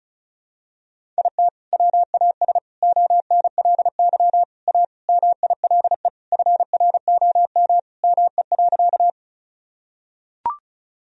Morse code, heard as a keyed tone, about 34 wpm, 700 Hz. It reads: IT WAS ONLY A MILE FROM ME.
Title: Sentences from Top 1000 Words - Review Part 1 - 35wpm